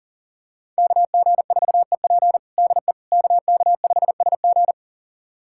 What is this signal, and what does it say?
Morse code, heard as a keyed tone, about 40 wpm, 700 Hz. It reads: KG4EP DE KK5SG
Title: USA Callsign DE Callsign - 5 Characters - 40wpm